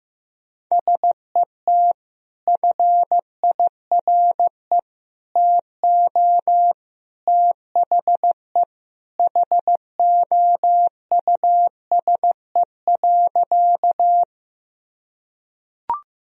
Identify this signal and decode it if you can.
Morse code, heard as a keyed tone, about 15 wpm, 700 Hz. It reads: SET FIRE TO THE HOUSE.
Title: Sentences from Top 1000 Words - Review Part 1 - 15wpm